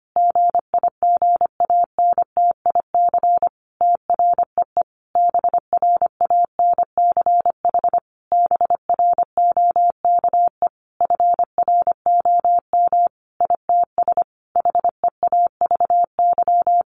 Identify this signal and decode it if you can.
Morse code, heard as a keyed tone, about 25 wpm, 700 Hz. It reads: GIGANTIC TREE BRANCH BROKE FROM ITS HEAVY
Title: A Sound Of Thunder 25wpm